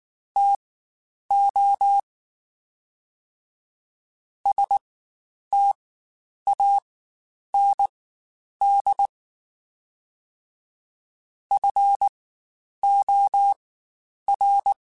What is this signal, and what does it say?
Morse code, heard as a keyed tone, about 19 wpm, spaced as for 9 wpm, 800 Hz. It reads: TO STAND FOR
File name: Quotes Spaced / Quotes Spaced-mp3s / StandFor sp @18